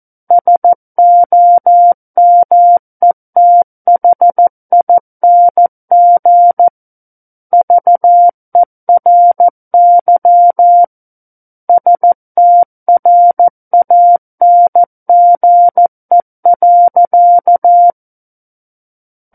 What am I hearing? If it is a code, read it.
Morse code, heard as a keyed tone, about 14 wpm, 700 Hz. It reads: SOMETHING VERY STRANGE.
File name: Book One / 05-Chapter 5 - 14 WPM